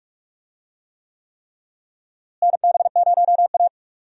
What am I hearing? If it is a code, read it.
Morse code, heard as a keyed tone, about 46 wpm, 700 Hz. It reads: NB0A
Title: USA Call Signs - 4 Characters - Repeated 3x - 45wpm